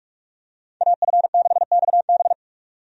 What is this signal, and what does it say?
Morse code, heard as a keyed tone, about 46 wpm, 700 Hz. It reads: AF6XB
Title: USA Call Signs - 5 Characters - 45wpm